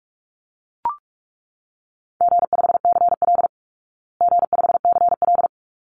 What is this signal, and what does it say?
Morse code, heard as a keyed tone, about 44 wpm, 700 Hz. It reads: G5CL G5CL
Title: Easy Call Signs Repeated 2x - 45wpm